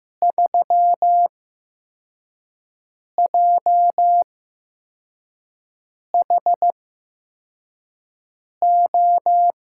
Morse code, heard as a keyed tone, about 15 wpm, spaced as for 4 wpm, 700 Hz. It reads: 3JHO
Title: USA Call Signs - 6 Characters - 8x Character Spacing - 15wpm